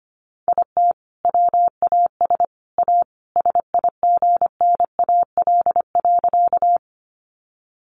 Morse code, heard as a keyed tone, about 25 wpm, 700 Hz. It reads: IT WAS A SIGNAL.
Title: Sentences from Top 1400 Words 25wpm